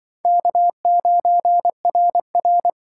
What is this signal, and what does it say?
Morse code, heard as a keyed tone, about 24 wpm, 700 Hz. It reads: K9RR
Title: Speed Racing - USA 1x2 Call Signs 20wpm